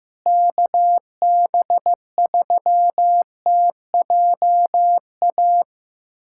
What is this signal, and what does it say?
Morse code, heard as a keyed tone, about 15 wpm, 700 Hz. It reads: KB3TJA
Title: USA Callsign DE Callsign - 6 Characters - 15wpm